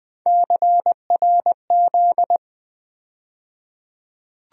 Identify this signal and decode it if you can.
Morse code, heard as a keyed tone, about 20 wpm, 700 Hz. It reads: CRZ
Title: USA Call Sign Postfixes - 3 Letters 20wpm